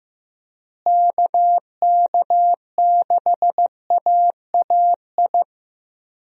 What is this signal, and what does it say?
Morse code, heard as a keyed tone, about 15 wpm, 700 Hz. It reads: KK6AAI
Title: USA Call Signs - 6 Characters - Repeated 3x - 15wpm